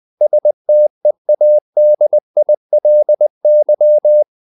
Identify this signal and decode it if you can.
Morse code, heard as a keyed tone, about 20 wpm, 600 Hz. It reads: STEADILY
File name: the old man and the sea part 2 20wpm